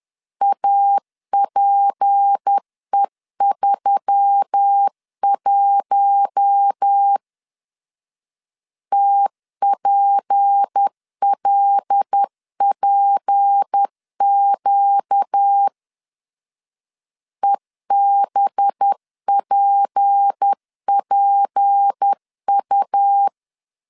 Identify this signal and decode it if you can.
Morse code, heard as a keyed tone, about 10 wpm, 800 Hz. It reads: APE31 TPLPQ EBPPU
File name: LEZIONI / 6 PKX0 / 6 P 10wpm 2432